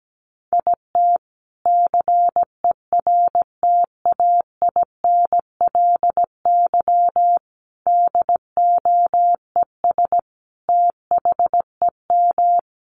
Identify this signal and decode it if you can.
Morse code, heard as a keyed tone, about 17 wpm, 700 Hz. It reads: IT CERTAINLY DOES THEM
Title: Sentences from Top 1000 Words 17wpm